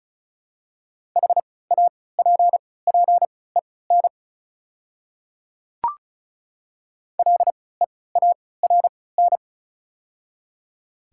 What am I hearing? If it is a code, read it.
Morse code, heard as a keyed tone, about 36 wpm, spaced as for 20 wpm, 700 Hz. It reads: HAPPEN LEARN
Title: 3x Character Spacing - Top 500 Words - 35wpm